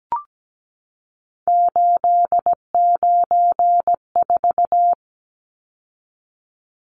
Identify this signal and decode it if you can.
Morse code, heard as a keyed tone, about 17 wpm, 700 Hz. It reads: 894
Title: Numbers - 3-Digit 17wpm